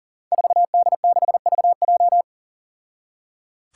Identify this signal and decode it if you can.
Morse code, heard as a keyed tone, about 37 wpm, 700 Hz. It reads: 4D6VJ
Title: Five Challenging Letter-Number Combinations 40wpm